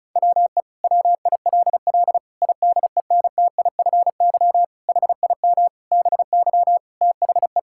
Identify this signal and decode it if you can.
Morse code, heard as a keyed tone, about 36 wpm, 700 Hz. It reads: WE WILL IDENTIFY HIM BY THE
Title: Sentences from Top 1000 Words - Review Part 6 - 35wpm